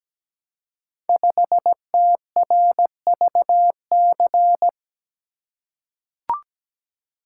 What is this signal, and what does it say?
Morse code, heard as a keyed tone, about 17 wpm, 700 Hz. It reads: 5TRVC
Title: Code Groups - 5 Letter:Numbers - 17wpm